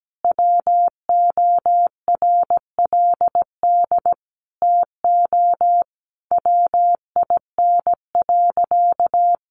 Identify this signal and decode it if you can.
Morse code, heard as a keyed tone, about 17 wpm, 700 Hz. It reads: WORLD TO WIN.
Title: Sentences from Top 1500 Words 17wpm